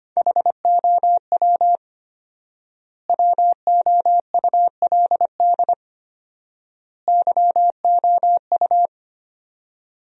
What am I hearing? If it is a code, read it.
Morse code, heard as a keyed tone, about 25 wpm, 700 Hz. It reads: HOW WOULD YOU
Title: 4x Word-Spacing - Sentences from Top 100 Words - 25wpm